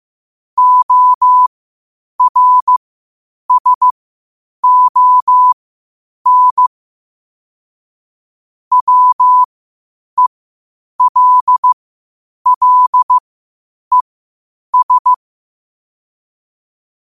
Morse code, heard as a keyed tone, about 15 wpm, spaced as for 8 wpm, 1000 Hz. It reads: ORSON WELLES
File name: qotdfiles / 8wpm